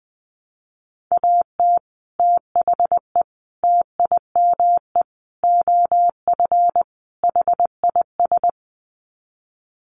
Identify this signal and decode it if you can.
Morse code, heard as a keyed tone, about 20 wpm, 700 Hz. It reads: AT THE TIME OF HIS
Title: Sets of 5 Words Encore2 20wpm